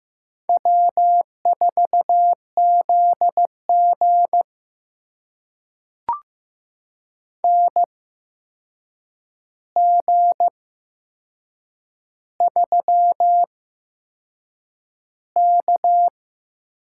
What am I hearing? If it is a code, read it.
Morse code, heard as a keyed tone, about 15 wpm, spaced as for 6 wpm, 700 Hz. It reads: W4ZG NG3K
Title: USA Call Signs - 4 Characters - 8x Character Spacing - 15wpm